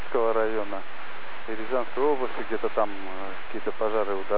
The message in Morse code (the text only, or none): none